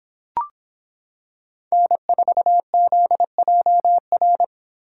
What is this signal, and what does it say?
Morse code, heard as a keyed tone, about 26 wpm, 700 Hz. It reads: N4ZJR